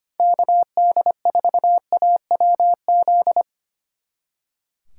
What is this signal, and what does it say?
Morse code, heard as a keyed tone, about 26 wpm, 700 Hz. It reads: KD4AWZ
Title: Speed Racing - USA 2x3 Call Signs 25wpm